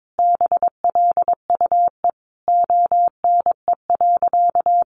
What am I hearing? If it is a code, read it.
Morse code, heard as a keyed tone, about 22 wpm, 700 Hz. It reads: BLUE ONE.
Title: Additional Easy Phrases 22wpm